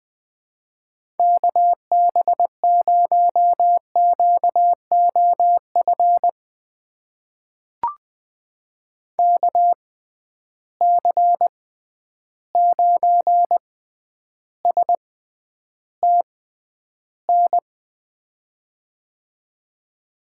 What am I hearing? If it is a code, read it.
Morse code, heard as a keyed tone, about 20 wpm, spaced as for 10 wpm, 700 Hz. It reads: KB0QOF KC9STN
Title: USA Call Signs - 6x Character Spacing - 20wpm